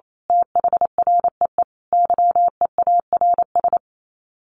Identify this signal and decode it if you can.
Morse code, heard as a keyed tone, about 28 wpm, 700 Hz. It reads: THREE YEARS
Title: Sets of 5 Words Encore2 28wpm